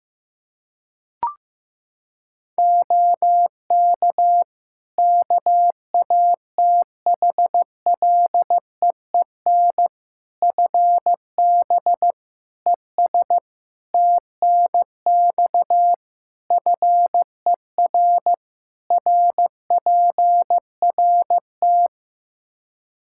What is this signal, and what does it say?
Morse code, heard as a keyed tone, about 15 wpm, 700 Hz. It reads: OK KATHLEEN FB ES TNX FER RPRT
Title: Practice QSO - Exchange 2 - Info - 15wpm